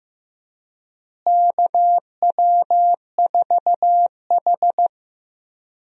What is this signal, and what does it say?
Morse code, heard as a keyed tone, about 16 wpm, 700 Hz. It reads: KW4H
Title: USA Call Signs - 4 Characters - Repeated 3x - 15wpm